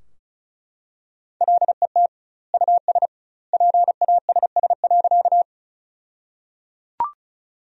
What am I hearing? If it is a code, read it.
Morse code, heard as a keyed tone, about 36 wpm, 700 Hz. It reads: LET US PASS.
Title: 2x Word-Spacing - Sentences from Top 400 Words - 35wpm